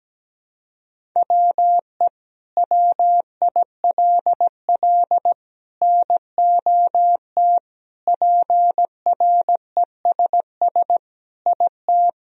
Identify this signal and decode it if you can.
Morse code, heard as a keyed tone, about 17 wpm, 700 Hz. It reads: WE WILL NOT PRESS IT
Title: Sentences from Top 1000 Words - Review Part 1 - 17wpm